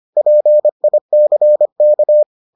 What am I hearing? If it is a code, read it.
Morse code, heard as a keyed tone, about 25 wpm, 600 Hz. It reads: PICK